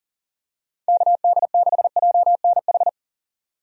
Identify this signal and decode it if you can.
Morse code, heard as a keyed tone, about 40 wpm, 700 Hz. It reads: KD6JNH
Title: USA Call Signs - 6 Characters - 40wpm